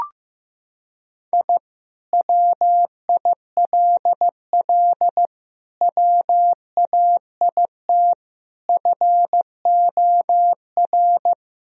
Morse code, heard as a keyed tone, about 16 wpm, 700 Hz. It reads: I WILL WAIT FOR
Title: Sentences from Top 1000 Words - Review Part 1 - 15wpm